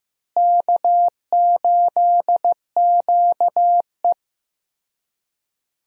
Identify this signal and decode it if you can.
Morse code, heard as a keyed tone, about 16 wpm, 700 Hz. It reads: K8QE
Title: USA Call Signs - 4 Characters - Repeated 2x - 15wpm